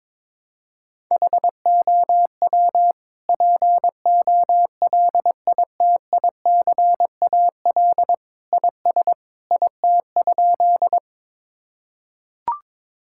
Morse code, heard as a keyed tone, about 22 wpm, 700 Hz. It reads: HOW POLITICAL IS IT?
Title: Sentences from Top 1000 Words - Review Part 5 - 22wpm